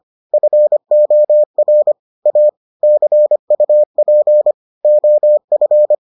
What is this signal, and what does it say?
Morse code, heard as a keyed tone, about 25 wpm, 600 Hz. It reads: FOR A CUP OF